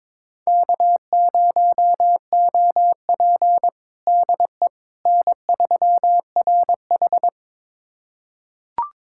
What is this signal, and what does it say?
Morse code, heard as a keyed tone, about 22 wpm, 700 Hz. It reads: K0OP DE N3RH
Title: USA Callsign DE Callsign - 4 Characters - 22wpm